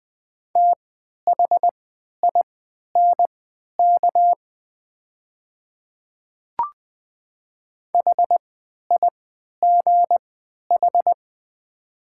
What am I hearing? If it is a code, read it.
Morse code, heard as a keyed tone, about 20 wpm, spaced as for 11 wpm, 690 Hz. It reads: THINK HIGH
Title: 3x Character Spacing - Top 300 Words - 20wpm